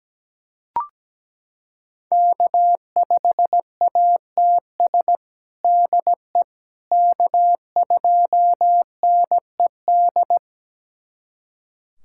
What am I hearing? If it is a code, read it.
Morse code, heard as a keyed tone, about 17 wpm, 700 Hz. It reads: K5ATS DE K2NED
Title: USA Callsign DE Callsign - 5 Characters - 17wpm